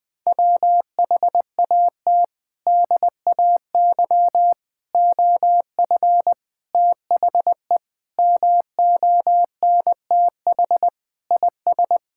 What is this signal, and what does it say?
Morse code, heard as a keyed tone, about 20 wpm, 700 Hz. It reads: WHAT DAY OF THE MONTH IS
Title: Sentences from Top 1000 Words - Review Part 5 - 20wpm